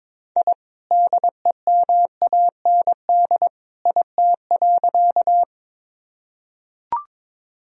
Morse code, heard as a keyed tone, about 22 wpm, 700 Hz. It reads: I DEMAND IT.
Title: Sentences from Top 1000 Words - Review Part 4 - 22wpm